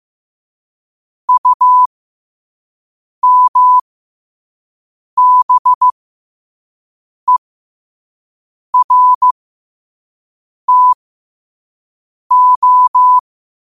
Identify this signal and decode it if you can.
Morse code, heard as a keyed tone, about 15 wpm, spaced as for 5 wpm, 1000 Hz. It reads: UMBERTO
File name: qotdfiles / 5wpm